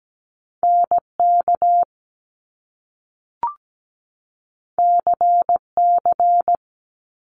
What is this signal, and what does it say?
Morse code, heard as a keyed tone, about 17 wpm, 700 Hz. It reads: NK CC
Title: Top Two Letter Combinations in Words 17wpm